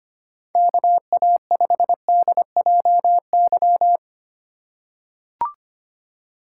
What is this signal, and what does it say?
Morse code, heard as a keyed tone, about 25 wpm, 700 Hz. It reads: KA5DJY
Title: USA Call Signs - 6 Characters - 3x Character Spacing - 25wpm